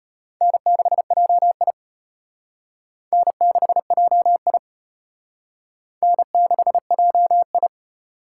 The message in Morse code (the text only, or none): N6JI N6JI N6JI